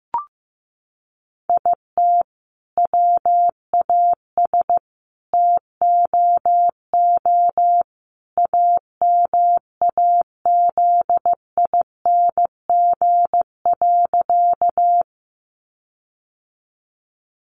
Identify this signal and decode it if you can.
Morse code, heard as a keyed tone, about 15 wpm, 700 Hz. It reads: IT WAS TOO AMAZING.
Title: Sentences from Top 1500 Words 15wpm